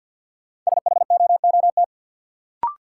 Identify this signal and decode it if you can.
Morse code, heard as a keyed tone, about 53 wpm, 700 Hz. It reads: SHOOT
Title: Sentences from Top 1200 Words - New Word Drill - 50wpm